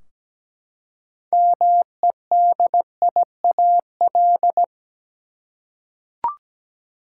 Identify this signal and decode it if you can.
Morse code, heard as a keyed tone, about 17 wpm, 700 Hz. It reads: MEDIAL
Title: K1USN SST QSO Practice 17wpm